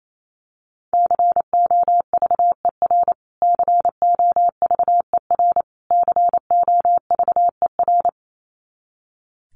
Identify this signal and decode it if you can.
Morse code, heard as a keyed tone, about 28 wpm, 700 Hz. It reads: COVER COVER COVER
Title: Top 500 Words Repeated 28wpm